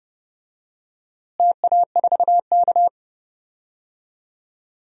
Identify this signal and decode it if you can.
Morse code, heard as a keyed tone, about 30 wpm, 700 Hz. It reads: TA4K
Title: DXCC-100 Call Signs - 4 Characters - Repeated 2x - 30wpm